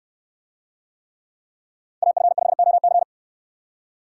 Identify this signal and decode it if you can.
Morse code, heard as a keyed tone, about 64 wpm, 700 Hz. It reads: AF5OC